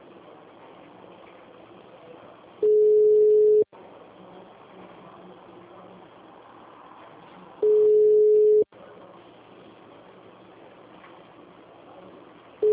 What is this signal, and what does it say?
Ringback tone (Europe, 425 Hz).